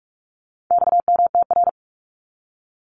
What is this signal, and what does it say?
Morse code, heard as a keyed tone, about 44 wpm, 700 Hz. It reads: XMTR
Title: Abbreviations 45wpm